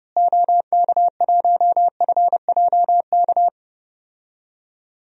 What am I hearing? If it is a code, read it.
Morse code, heard as a keyed tone, about 31 wpm, 700 Hz. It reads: OK1FJK
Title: DXCC-100 Call Signs - 6 Characters - Repeated 3x - 30wpm